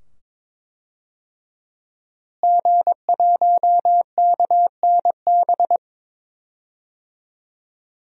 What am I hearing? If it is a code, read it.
Morse code, heard as a keyed tone, about 23 wpm, 700 Hz. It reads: G1KNB
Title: DXCC-100 Call Signs - 5 Characters - Repeated 2x - 22wpm